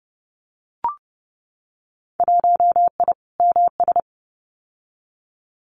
Morse code, heard as a keyed tone, about 30 wpm, 700 Hz. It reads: 1I MS
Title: Winter Field Day Exchanges Encore 30wpm